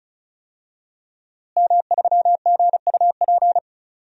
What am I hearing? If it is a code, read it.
Morse code, heard as a keyed tone, about 36 wpm, 700 Hz. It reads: M3GUP
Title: DXCC-100 Call Signs - 5 Characters - Repeated 2x - 35wpm